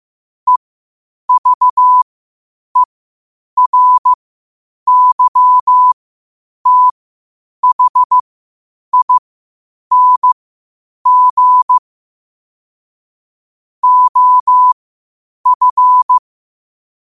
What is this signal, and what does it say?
Morse code, heard as a keyed tone, about 15 wpm, spaced as for 8 wpm, 1000 Hz. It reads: EVERYTHING OF
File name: qotdfiles / 8wpm